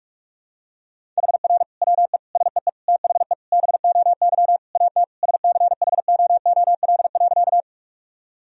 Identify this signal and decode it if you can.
Morse code, heard as a keyed tone, about 46 wpm, 700 Hz. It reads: HR WE SEE THE BOY AT SCHOOL.
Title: Sentences from Top 300 Words and Abbreviations 45wpm